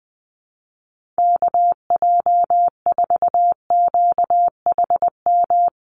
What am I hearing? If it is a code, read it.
Morse code, heard as a keyed tone, about 20 wpm, 700 Hz. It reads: KJ4QHM
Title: USA Call Signs Repeated 2x 20wpm